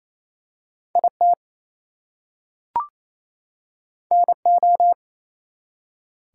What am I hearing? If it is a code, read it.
Morse code, heard as a keyed tone, about 28 wpm, 700 Hz. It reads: IT NO
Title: Two Letter Words - 28wpm